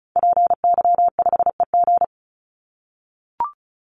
Morse code, heard as a keyed tone, about 34 wpm, 700 Hz. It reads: PY5EG
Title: Call Signs - 35wpm